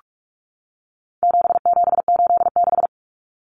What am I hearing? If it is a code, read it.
Morse code, heard as a keyed tone, about 44 wpm, 700 Hz. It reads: 7786
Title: Numbers - 4-Digit 45wpm